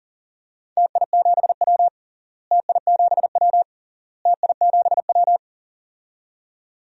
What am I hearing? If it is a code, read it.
Morse code, heard as a keyed tone, about 40 wpm, 700 Hz. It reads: TI7W TI7W TI7W
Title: DXCC-100 Call Signs - 4 Characters - Repeated 3x - 40wpm